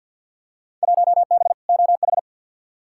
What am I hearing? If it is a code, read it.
Morse code, heard as a keyed tone, about 52 wpm, 700 Hz. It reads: 1B OH
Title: Field Day Exchanges - 50wpm